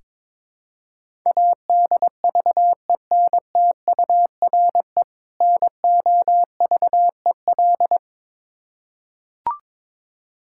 Morse code, heard as a keyed tone, about 21 wpm, 700 Hz. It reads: ADVENTURE NOVEL
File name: Difficult Words with Hints - 22wpm